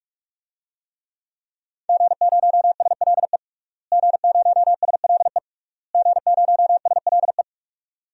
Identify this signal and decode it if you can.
Morse code, heard as a keyed tone, about 46 wpm, 700 Hz. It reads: G0SLE G0SLE G0SLE
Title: DXCC-100 Call Signs - 5 Characters - Repeated 3x - 45wpm